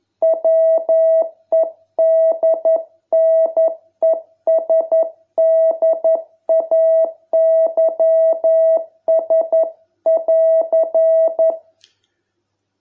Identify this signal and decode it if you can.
Morse code, heard as a keyed tone, about 11 wpm, 660 Hz. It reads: WEDNESDAYS<AR>